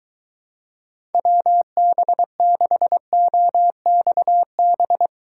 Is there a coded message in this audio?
Yes: Morse code from 1.1 to 5.1 s, heard as a keyed tone, about 22 wpm, 700 Hz. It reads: WB6OXB